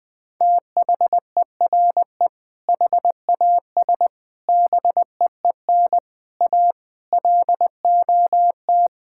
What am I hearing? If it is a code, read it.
Morse code, heard as a keyed tone, about 20 wpm, 700 Hz. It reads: THERE HAS BEEN A LOT
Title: Sentences with 7-Word Phrases - 20wpm